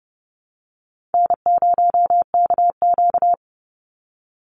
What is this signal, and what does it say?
Morse code, heard as a keyed tone, about 30 wpm, 700 Hz. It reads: N0KQ
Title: Easy Call Signs Repeated 3x - 30wpm